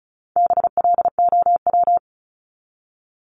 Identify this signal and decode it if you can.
Morse code, heard as a keyed tone, about 34 wpm, 700 Hz. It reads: BLOW
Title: Top 1751-2000 Words 35wpm